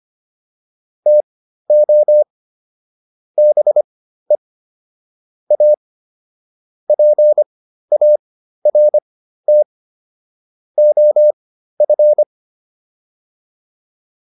Morse code, heard as a keyed tone, about 25 wpm, spaced as for 13 wpm, 600 Hz. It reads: TO BE A PART OF